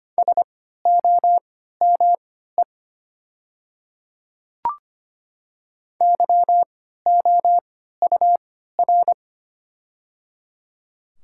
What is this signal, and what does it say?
Morse code, heard as a keyed tone, about 25 wpm, spaced as for 14 wpm, 690 Hz. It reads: SOME YOUR
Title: 3x Character Spacing - Top 100 Words - 25wpm